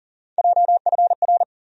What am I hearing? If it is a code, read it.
Morse code, heard as a keyed tone, about 40 wpm, 700 Hz. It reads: JFR
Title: USA Call Sign Postfixes - 3 Letters 40wpm